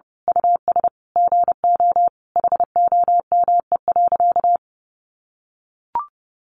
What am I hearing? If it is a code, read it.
Morse code, heard as a keyed tone, about 30 wpm, 700 Hz. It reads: US GO HOME.
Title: Sentences from Top 300 Words 30wpm